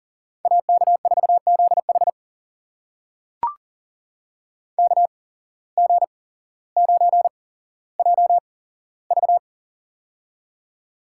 Morse code, heard as a keyed tone, about 40 wpm, spaced as for 17 wpm, 700 Hz. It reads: AK4ZH KG9JV
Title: USA Call Signs - 5 Characters - 8x Character Spacing - 40wpm